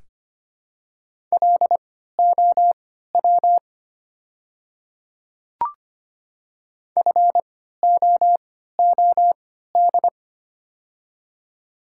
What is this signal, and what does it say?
Morse code, heard as a keyed tone, about 25 wpm, spaced as for 14 wpm, 690 Hz. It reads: LOW FOOD
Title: 3x Character Spacing - Top 300 Words - 25wpm